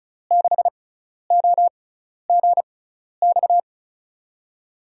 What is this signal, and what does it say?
Morse code, heard as a keyed tone, about 34 wpm, spaced as for 12 wpm, 700 Hz. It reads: 6OGX
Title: USA Call Signs - 6 Characters - 6x Character Spacing - 35wpm